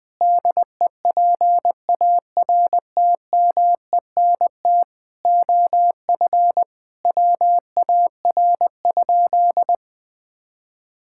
Morse code, heard as a keyed tone, about 20 wpm, 700 Hz. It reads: DEPARTMENT OF WAR?
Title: Sentences from Top 1000 Words - Review Part 5 - 20wpm